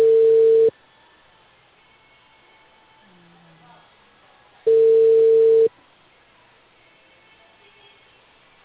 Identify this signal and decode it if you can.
Ringback tone (China, 450 Hz), with speech.